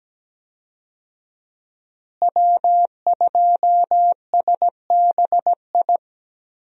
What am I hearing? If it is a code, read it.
Morse code, heard as a keyed tone, about 17 wpm, 700 Hz. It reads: W2SBI
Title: USA Call Signs - 5 Characters - Repeated 3x - 17wpm